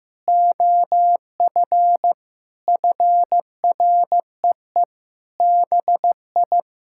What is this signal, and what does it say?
Morse code, heard as a keyed tone, about 15 wpm, 700 Hz. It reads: OF FREE BI
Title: Sentences from Top 1000 Words - Review Part 5 - 15wpm